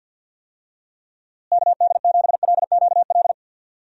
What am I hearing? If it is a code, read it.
Morse code, heard as a keyed tone, about 48 wpm, 700 Hz. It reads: KD7LQL
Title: USA Call Signs - 6 Characters - Repeated 2x - 50wpm